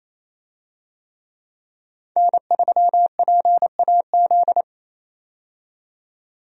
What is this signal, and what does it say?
Morse code, heard as a keyed tone, about 28 wpm, 700 Hz. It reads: N3PAZ
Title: USA Call Signs - 5 Characters - 28wpm